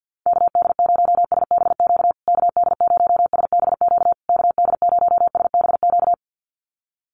Morse code, heard as a keyed tone, about 49 wpm, 700 Hz. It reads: KD0SBQ KD0SBQ KD0SBQ
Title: USA Call Signs Repeated 3x 50wpm